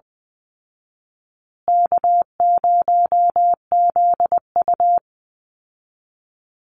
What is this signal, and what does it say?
Morse code, heard as a keyed tone, about 20 wpm, 700 Hz. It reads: K0ZU
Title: Easy Call Signs Repeated 3x - 20wpm